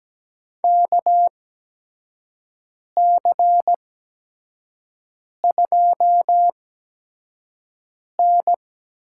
Morse code, heard as a keyed tone, about 17 wpm, spaced as for 5 wpm, 700 Hz. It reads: KC2N
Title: USA Call Signs - 6 Characters - 8x Character Spacing - 17wpm